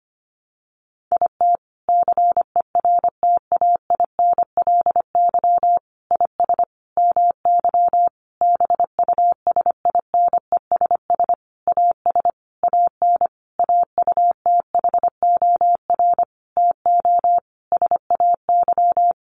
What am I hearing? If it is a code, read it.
Morse code, heard as a keyed tone, about 25 wpm, 700 Hz. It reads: IT CERTAINLY IS MY BUSINESS AS AN AUTHOR TO SAY
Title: Sentences from Top 1200 Words 25wpm